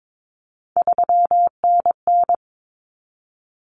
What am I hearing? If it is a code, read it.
Morse code, heard as a keyed tone, about 22 wpm, 700 Hz. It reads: 3NN
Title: Common Cut Numbers - 3-Digit 22wpm